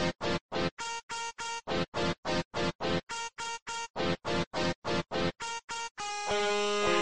There is no code.